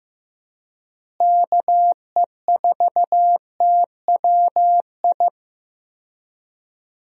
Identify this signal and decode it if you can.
Morse code, heard as a keyed tone, about 15 wpm, 700 Hz. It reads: KE4TWI
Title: USA Call Signs - 6 Characters - Repeated 3x - 15wpm